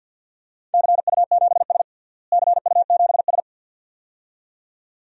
Morse code, heard as a keyed tone, about 52 wpm, 700 Hz. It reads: CU7S CU7S
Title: DXCC-100 Call Signs - 4 Characters - Repeated 2x - 50wpm